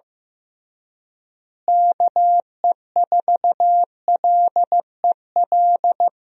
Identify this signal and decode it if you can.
Morse code, heard as a keyed tone, about 16 wpm, 700 Hz. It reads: KE4LEL
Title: USA Call Signs - 6 Characters - Repeated 3x - 15wpm